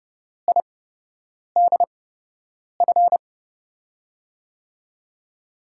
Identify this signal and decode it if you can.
Morse code, heard as a keyed tone, about 31 wpm, spaced as for 8 wpm, 700 Hz. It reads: IDF
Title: USA Call Signs - 6 Characters - 8x Character Spacing - 30wpm